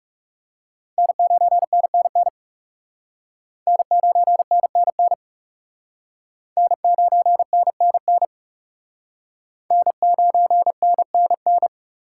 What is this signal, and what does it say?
Morse code, heard as a keyed tone, about 36 wpm, 700 Hz. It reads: N9NNN N9NNN N9NNN N9NNN
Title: Speed Racing - USA 1x3 Call Signs 30wpm